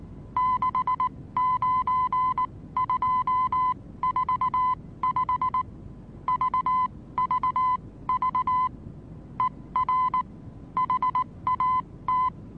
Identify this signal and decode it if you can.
Morse code, heard as a keyed tone, about 19 wpm, spaced as for 16 wpm, 1010 Hz. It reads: 69245 VVV ER HAT
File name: repos / Morsekurs / DJ4UF Morsen26